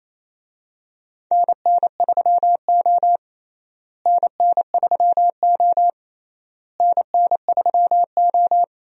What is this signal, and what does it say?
Morse code, heard as a keyed tone, about 27 wpm, 700 Hz. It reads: NN3O NN3O NN3O
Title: USA Call Signs - 4 Characters - Repeated 3x - 28wpm